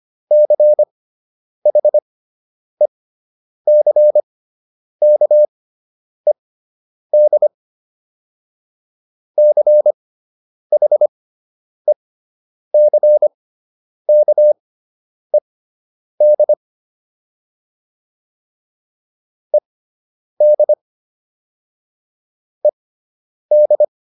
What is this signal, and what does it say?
Morse code, heard as a keyed tone, about 25 wpm, spaced as for 9 wpm, 600 Hz. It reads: CHECKED CHECKED ED ED